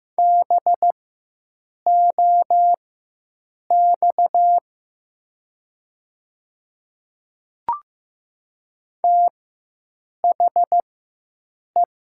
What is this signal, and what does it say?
Morse code, heard as a keyed tone, about 15 wpm, spaced as for 7 wpm, 700 Hz. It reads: BOX THE
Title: 4x Character Spacing - Top 500 Words - 15wpm